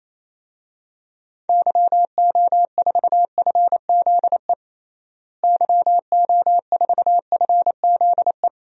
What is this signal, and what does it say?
Morse code, heard as a keyed tone, about 28 wpm, 700 Hz. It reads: YO4FZE YO4FZE
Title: DXCC-100 Call Signs - 6 Characters - Repeated 2x - 28wpm